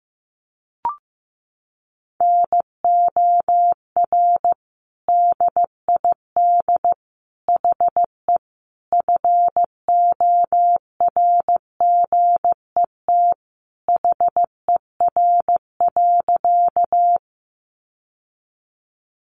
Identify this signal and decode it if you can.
Morse code, heard as a keyed tone, about 15 wpm, 700 Hz. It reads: NOR DID HE FORGET HER.
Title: Sentences from Top 1500 Words 15wpm